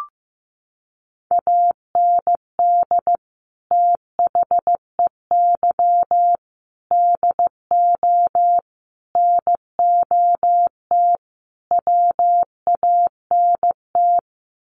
Morse code, heard as a keyed tone, about 15 wpm, 700 Hz. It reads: AND THEY DO NOT WANT
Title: Sets of 5 Words Encore2 15wpm